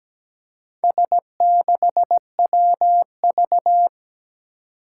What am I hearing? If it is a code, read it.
Morse code, heard as a keyed tone, about 17 wpm, 700 Hz. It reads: S6WV
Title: Four Challenging Letter-Number Combinations 17wpm